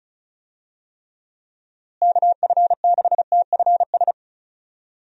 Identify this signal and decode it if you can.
Morse code, heard as a keyed tone, about 34 wpm, 700 Hz. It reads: KF6TFS
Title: USA Call Signs - 6 Characters - Repeated 2x - 35wpm